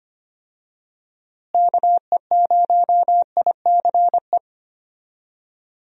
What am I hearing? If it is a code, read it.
Morse code, heard as a keyed tone, about 25 wpm, 700 Hz. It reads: KE0ICE
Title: USA Call Signs - 6 Characters - Repeated 3x - 25wpm